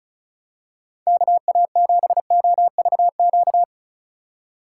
Morse code, heard as a keyed tone, about 34 wpm, 700 Hz. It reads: KA7OVQ
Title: USA Call Signs - 6 Characters - 3x Character Spacing - 35wpm